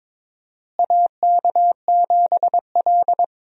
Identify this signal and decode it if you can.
Morse code, heard as a keyed tone, about 22 wpm, 700 Hz. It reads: AK7L
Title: USA Call Signs - 4 Characters - Repeated 3x - 22wpm